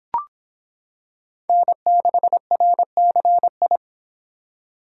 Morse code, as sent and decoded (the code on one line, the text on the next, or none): -. -.... .-. -.-. ..
N6RCI